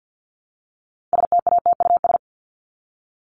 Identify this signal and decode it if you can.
Morse code, heard as a keyed tone, about 50 wpm, 700 Hz. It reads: STATUS
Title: Top 1251-1500 Words 50wpm